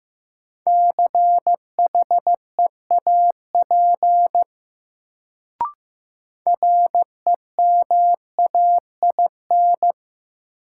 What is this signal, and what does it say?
Morse code, heard as a keyed tone, about 15 wpm, 700 Hz. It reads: CHEAP REMAIN
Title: Sentences from Top 1200 Words - New Word Drill Alt - 15wpm